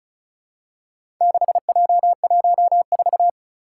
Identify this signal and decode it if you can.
Morse code, heard as a keyed tone, about 36 wpm, 700 Hz. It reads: 6J14
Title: Four Challenging Letter-Number Combinations 35wpm